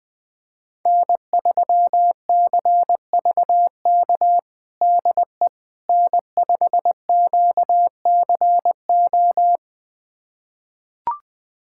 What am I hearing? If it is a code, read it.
Morse code, heard as a keyed tone, about 20 wpm, 700 Hz. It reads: N3CVK DE N5QCO
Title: USA Callsign DE Callsign - 5 Characters - 20wpm